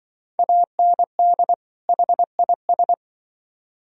Morse code, heard as a keyed tone, about 25 wpm, 700 Hz. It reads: AND HIS
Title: Speed-Racing - Sets of 2 Words 20wpm